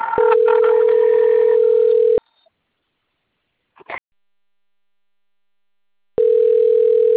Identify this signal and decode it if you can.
Ringback tone (North America), 440 + 480 Hz.